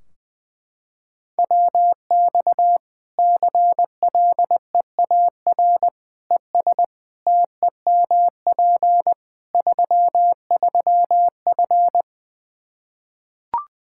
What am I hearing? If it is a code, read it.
Morse code, heard as a keyed tone, about 20 wpm, 700 Hz. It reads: WX CLEAR ES TEMP 33F
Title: Practice QSO - Exchange 2 - Info - 20wpm